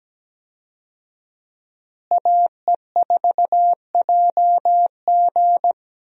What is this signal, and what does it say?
Morse code, heard as a keyed tone, about 17 wpm, 700 Hz. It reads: AE4JG
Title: USA Call Signs - 5 Characters - Repeated 3x - 17wpm